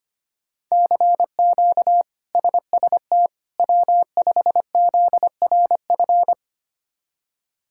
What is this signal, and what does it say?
Morse code, heard as a keyed tone, about 25 wpm, 700 Hz. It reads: CQ SST W5ZRF
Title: K1USN SST QSO Practice - 25wpm